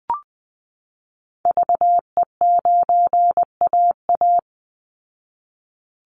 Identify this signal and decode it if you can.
Morse code, heard as a keyed tone, about 20 wpm, 700 Hz. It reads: VE9AA
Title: Call Signs Repeated 3x - 20wpm